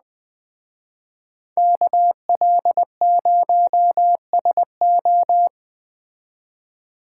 Morse code, heard as a keyed tone, about 20 wpm, 700 Hz. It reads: KL0SO
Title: Speed Racing - USA 2x2 Call Signs 17wpm